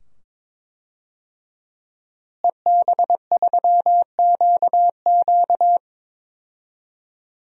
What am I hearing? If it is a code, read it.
Morse code, heard as a keyed tone, about 22 wpm, 700 Hz. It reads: EB3QQ
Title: DXCC-100 Call Signs - 5 Characters - Repeated 2x - 22wpm